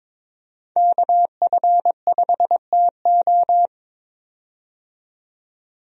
Morse code, heard as a keyed tone, about 22 wpm, 700 Hz. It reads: KF5TO
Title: USA Call Signs - 5 Characters - 22wpm